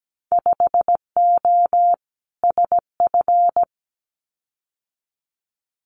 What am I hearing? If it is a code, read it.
Morse code, heard as a keyed tone, about 17 wpm, 700 Hz. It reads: 5O SF
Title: Winter Field Day Exchanges Encore 17wpm